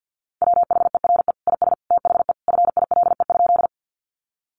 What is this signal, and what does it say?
Morse code, heard as a keyed tone, about 50 wpm, 700 Hz. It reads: WHERE IS THE FILE?
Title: Sentences from Top 1200 Words 50wpm